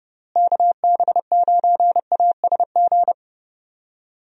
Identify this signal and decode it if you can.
Morse code, heard as a keyed tone, about 30 wpm, 700 Hz. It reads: KB9ASG
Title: USA Call Signs - 6 Characters - Repeated 3x - 30wpm